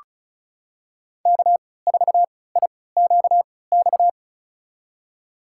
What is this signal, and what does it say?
Morse code, heard as a keyed tone, about 36 wpm, spaced as for 20 wpm, 700 Hz. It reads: K4IQX
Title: USA Call Signs - 5 Characters - 3x Character Spacing - 35wpm